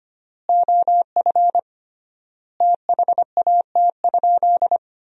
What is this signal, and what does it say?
Morse code, heard as a keyed tone, about 25 wpm, 700 Hz. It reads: OF THAT?
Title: 3x Word-Spacing - Sentences from Top 100 Words - 25wpm